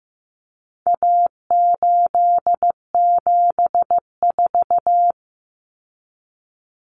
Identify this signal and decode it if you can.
Morse code, heard as a keyed tone, about 15 wpm, 700 Hz. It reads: A874
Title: Common Cut Numbers - 4-Digit 15wpm